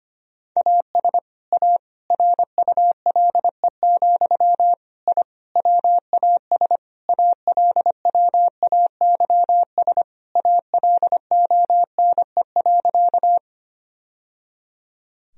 Morse code, heard as a keyed tone, about 25 wpm, 700 Hz. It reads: AS A RULE, I WAS ALWAYS ALONE.
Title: Sentences from Top 1000 Words - Review Part 2 - 25wpm